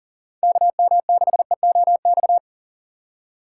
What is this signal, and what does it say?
Morse code, heard as a keyed tone, about 52 wpm, 700 Hz. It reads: KM6EOX